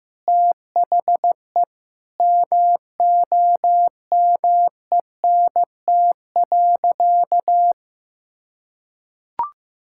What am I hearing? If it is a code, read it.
Morse code, heard as a keyed tone, about 16 wpm, 700 Hz. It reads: THE MOMENT.
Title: Sentences from Top 1000 Words - Review Part 6 - 15wpm